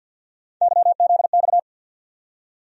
Morse code, heard as a keyed tone, about 47 wpm, 700 Hz. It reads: YZX
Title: USA Call Sign Postfixes - 3 Letters 50wpm